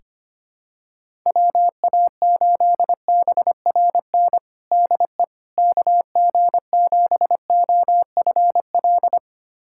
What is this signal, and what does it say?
Morse code, heard as a keyed tone, about 25 wpm, 700 Hz. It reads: WA8BRN DE KG7OFL
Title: USA Callsign DE Callsign - 6 Characters - 25wpm